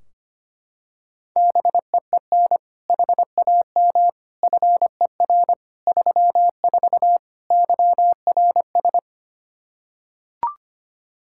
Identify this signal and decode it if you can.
Morse code, heard as a keyed tone, about 25 wpm, 700 Hz. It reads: BEEN HAM FER 34 YRS
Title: Practice QSO - Exchange 3 - More Info - 25wpm